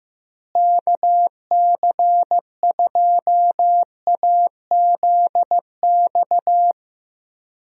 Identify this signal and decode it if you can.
Morse code, heard as a keyed tone, about 16 wpm, 700 Hz. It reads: KC2AZX
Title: USA Call Signs - 6 Characters - Repeated 3x - 15wpm